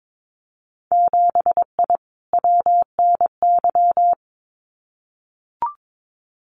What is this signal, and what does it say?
Morse code, heard as a keyed tone, about 22 wpm, 700 Hz. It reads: 7I WNY
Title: Winter Field Day Exchanges Encore 22wpm